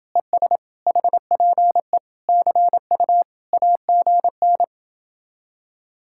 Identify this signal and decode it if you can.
Morse code, heard as a keyed tone, about 29 wpm, 700 Hz. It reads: ES HPE CU AGN